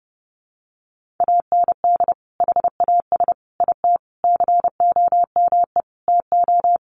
Morse code, heard as a keyed tone, about 30 wpm, 700 Hz. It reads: AND HAS IT COME TO
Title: Sentences from Top 100 Words 30wpm